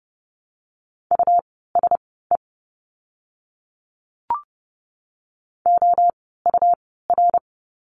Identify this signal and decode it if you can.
Morse code, heard as a keyed tone, about 30 wpm, spaced as for 17 wpm, 700 Hz. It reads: USE OUR
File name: Lesson 17f - Letter u - Words 30wpm